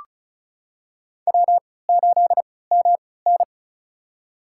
Morse code, heard as a keyed tone, about 36 wpm, spaced as for 20 wpm, 700 Hz. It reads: W8MN
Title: USA Call Signs - 4 Characters - 3x Character Spacing - 35wpm